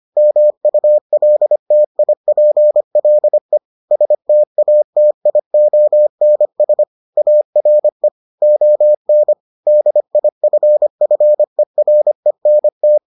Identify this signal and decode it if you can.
Morse code, heard as a keyed tone, about 25 wpm, 600 Hz. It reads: MULTIPLE STATIONS ARE ON DIFFERENT